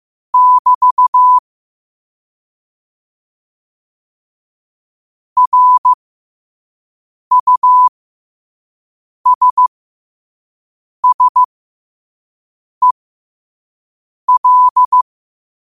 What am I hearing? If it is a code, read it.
Morse code, heard as a keyed tone, about 15 wpm, spaced as for 5 wpm, 1000 Hz. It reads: <BT> RUSSEL
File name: qotdfiles / 5wpm